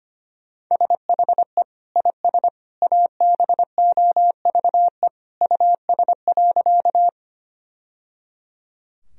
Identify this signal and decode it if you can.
Morse code, heard as a keyed tone, about 25 wpm, 700 Hz. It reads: SHE IS ABOVE US.
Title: Sentences from Top 1000 Words - Review Part 2 - 25wpm